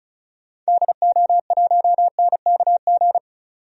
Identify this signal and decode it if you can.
Morse code, heard as a keyed tone, about 36 wpm, 700 Hz. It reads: DO1NKG
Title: DXCC-100 Call Signs - 6 Characters - 35wpm